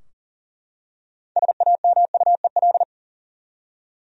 Morse code, heard as a keyed tone, about 40 wpm, 700 Hz. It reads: SAMUEL
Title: USA Common Names - 40wpm